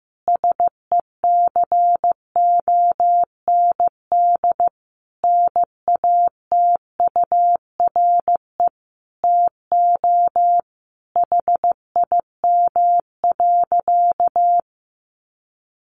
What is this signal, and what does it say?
Morse code, heard as a keyed tone, about 14 wpm, 700 Hz. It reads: SECOND NATURE TO HIM.
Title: Sentences from Top 600 Words 15wpm